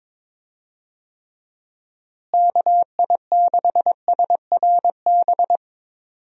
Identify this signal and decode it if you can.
Morse code, heard as a keyed tone, about 22 wpm, 700 Hz. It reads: KI6SRB
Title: USA Call Signs - 6 Characters - Repeated 3x - 22wpm